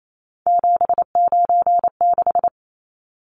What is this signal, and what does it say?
Morse code, heard as a keyed tone, about 28 wpm, 700 Hz. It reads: 796
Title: Numbers - 3-Digit 28wpm